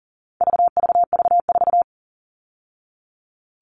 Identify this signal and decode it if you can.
Morse code, heard as a keyed tone, about 40 wpm, 700 Hz. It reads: VVV4
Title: All Cut Numbers - 4-Digit 40wpm